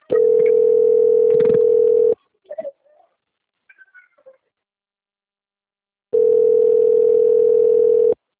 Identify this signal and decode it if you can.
Ringback tone (North America), 440 + 480 Hz.
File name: out- - - - - .55873